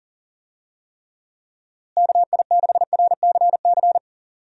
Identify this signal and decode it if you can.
Morse code, heard as a keyed tone, about 40 wpm, 700 Hz. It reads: KI6RCC
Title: USA Call Signs - 6 Characters - Repeated 2x - 40wpm